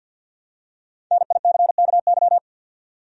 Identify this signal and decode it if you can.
Morse code, heard as a keyed tone, about 53 wpm, 700 Hz. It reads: NICKY
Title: USA Common Names - 50wpm